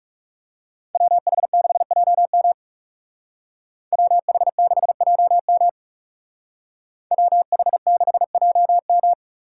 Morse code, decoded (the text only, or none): WH6JM WH6JM WH6JM